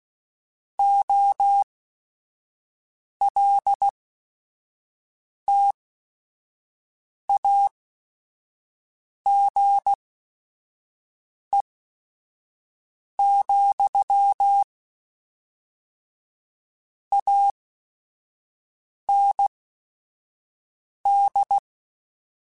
Morse code, heard as a keyed tone, about 16 wpm, spaced as for 5 wpm, 800 Hz. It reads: OLTAGE, AND
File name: Technical Subjects / Technical Subjects-mp3s / XC @5